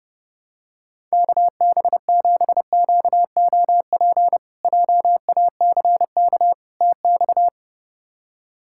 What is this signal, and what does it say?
Morse code, heard as a keyed tone, about 32 wpm, 700 Hz. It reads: KB7QOP JACK TX
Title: K1USN SST QSO Practice - 30wpm